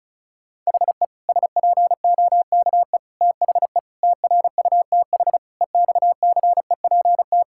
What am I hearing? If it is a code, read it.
Morse code, heard as a keyed tone, about 34 wpm, 700 Hz. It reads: HE SPOKE THE TRUTH EXCEPT
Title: Sentences from Top 1600 Words 35wpm